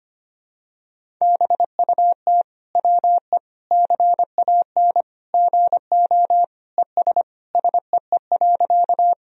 Morse code, heard as a keyed tone, about 25 wpm, 700 Hz. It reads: BUT WE CAN GO ES SEE.
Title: Sentences from Top 100 Words and Abbreviations 25wpm